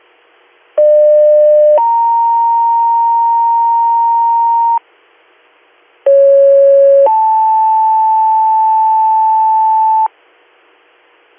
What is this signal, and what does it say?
Two-tone page, heard as a one-second tone, then a three-second tone, about 600 then 930 Hz, about 550 then 880 Hz. It is heard from 0.8 to 10.1 s.